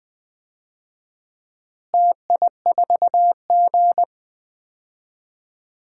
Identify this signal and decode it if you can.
Morse code, heard as a keyed tone, about 20 wpm, 700 Hz. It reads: TI4G
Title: DXCC-100 Call Signs - 4 Characters - Repeated 3x - 20wpm